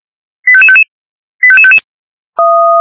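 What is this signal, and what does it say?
Five-tone selective call (ZVEI), heard as quick runs of five tones, tones of about 70 ms. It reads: 85555 85555
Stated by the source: ZVEI five-tone call received by SDR.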